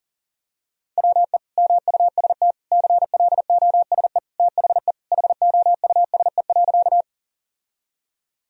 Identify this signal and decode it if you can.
Morse code, heard as a keyed tone, about 37 wpm, 700 Hz. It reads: WE MUST CLOSE THE HOUSE.
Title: Sentences from Top 1000 Words - Review Part 1 - 40wpm